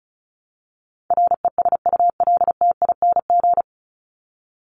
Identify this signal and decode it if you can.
Morse code, heard as a keyed tone, about 34 wpm, 700 Hz. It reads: RESULTING
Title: ING Suffix - Top 100 Words with Base Words 35wpm